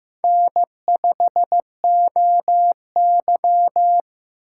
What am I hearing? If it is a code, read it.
Morse code, heard as a keyed tone, about 15 wpm, 700 Hz. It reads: N5OY
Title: USA Call Signs - 4x Character Spacing - 15wpm